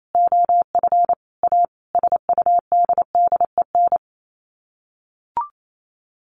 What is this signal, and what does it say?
Morse code, heard as a keyed tone, about 28 wpm, 700 Hz. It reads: OF A SUDDEN
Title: Sets of 5 Words 28wpm